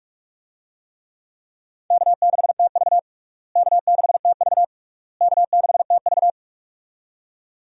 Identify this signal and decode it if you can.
Morse code, heard as a keyed tone, about 46 wpm, 700 Hz. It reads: K6TV K6TV K6TV
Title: USA Call Signs - 4 Characters - Repeated 3x - 45wpm